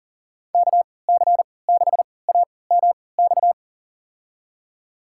Morse code, heard as a keyed tone, about 40 wpm, spaced as for 23 wpm, 700 Hz. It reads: KC6AMX
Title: USA Call Signs - 6 Characters - 3x Character Spacing - 40wpm